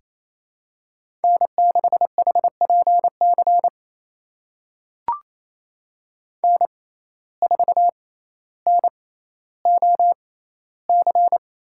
Morse code, heard as a keyed tone, about 28 wpm, spaced as for 14 wpm, 700 Hz. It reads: N6HPC N4NOC
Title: USA Call Signs - 5 Characters - 6x Character Spacing - 28wpm